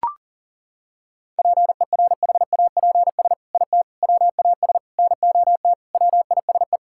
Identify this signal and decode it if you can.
Morse code, heard as a keyed tone, about 37 wpm, 700 Hz. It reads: PERHAPS IT WAS NOT WISE
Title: Sentences from Top 1600 Words 40wpm